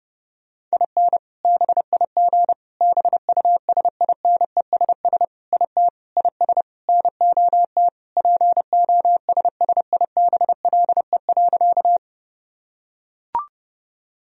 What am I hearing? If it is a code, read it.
Morse code, heard as a keyed tone, about 32 wpm, 700 Hz. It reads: IN BIG BUSINESS IT IS NOT POSSIBLE.
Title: Sentences from Top 1000 Words - Review Part 1 - 30wpm